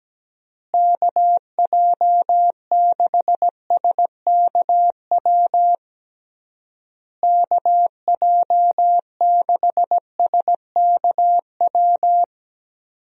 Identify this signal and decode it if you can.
Morse code, heard as a keyed tone, about 17 wpm, 700 Hz. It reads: KJ6SKW KJ6SKW
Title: USA Call Signs - 6 Characters - Repeated 3x - 17wpm